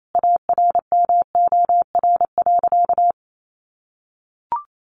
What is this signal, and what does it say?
Morse code, heard as a keyed tone, about 28 wpm, 700 Hz. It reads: ARMOR.
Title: A Sound Of Thunder 28wpm